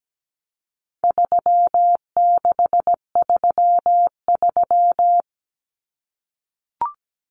Morse code, heard as a keyed tone, about 17 wpm, 700 Hz. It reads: 3633
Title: Numbers - 4-Digit 17wpm